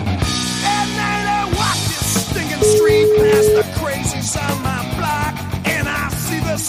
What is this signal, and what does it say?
Ringback tone (Europe, 425 Hz), with noise.